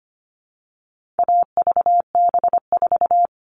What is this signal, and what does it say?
Morse code, heard as a keyed tone, about 25 wpm, 700 Hz. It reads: AVB4
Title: All Cut Numbers - 4-Digit 25wpm